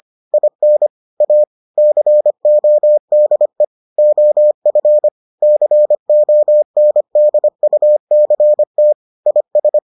Morse code, heard as a keyed tone, about 25 wpm, 600 Hz. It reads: IN A CODE OF CONDUCT IS